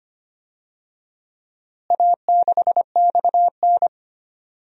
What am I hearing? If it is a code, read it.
Morse code, heard as a keyed tone, about 25 wpm, 700 Hz. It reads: A6XN
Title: DXCC-100 Call Signs - 4 Characters - Repeated 3x - 25wpm